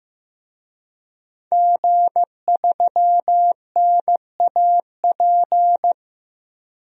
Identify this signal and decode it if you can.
Morse code, heard as a keyed tone, about 15 wpm, 700 Hz. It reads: G3NAP
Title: DXCC-100 Call Signs - 5 Characters - Repeated 3x - 15wpm